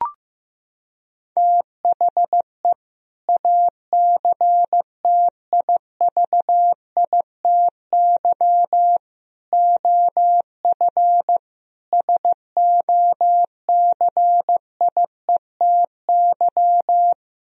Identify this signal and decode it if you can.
Morse code, heard as a keyed tone, about 16 wpm, 700 Hz. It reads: THE ACTIVITY OF SOCIETY
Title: Sentences from Top 1000 Words - Review Part 4 - 15wpm